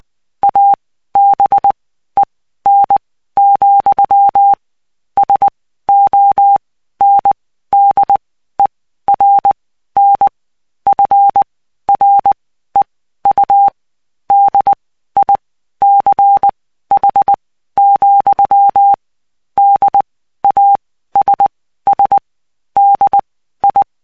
Morse code, heard as a keyed tone, about 20 wpm, spaced as for 13 wpm, 800 Hz. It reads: ABEN, SONDERN FREU DICH, DASS DI
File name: cw 75BPM